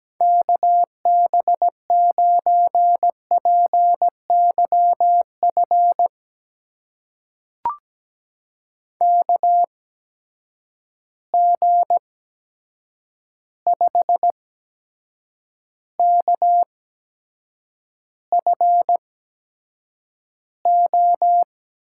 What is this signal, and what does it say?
Morse code, heard as a keyed tone, about 17 wpm, spaced as for 7 wpm, 700 Hz. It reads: KB9PYF KG5KFO
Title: USA Call Signs - 6 Characters - 8x Character Spacing - 17wpm